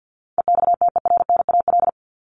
Morse code, heard as a keyed tone, about 50 wpm, 700 Hz. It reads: EXTERNAL
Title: Top 1751-2000 Words 50wpm